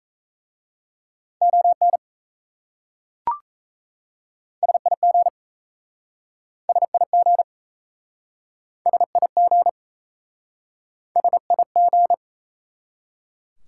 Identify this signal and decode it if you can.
Morse code, heard as a keyed tone, about 37 wpm, 700 Hz. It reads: ON SIG SIG SIG SIG